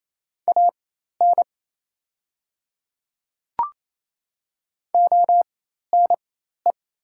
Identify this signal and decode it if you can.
Morse code, heard as a keyed tone, about 28 wpm, spaced as for 13 wpm, 700 Hz. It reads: AN ONE
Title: 4x Character Spacing - Top 100 Words - 28wpm